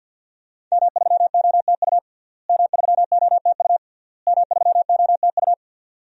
Morse code, heard as a keyed tone, about 46 wpm, 700 Hz. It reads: M3OTU M3OTU M3OTU
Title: DXCC-100 Call Signs - 5 Characters - Repeated 3x - 50wpm